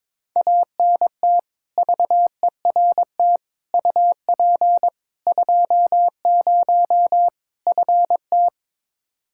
Morse code, heard as a keyed tone, about 22 wpm, 700 Hz. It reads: ANT VERT UP 20 FT
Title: K1USN SST QSO Practice 22wpm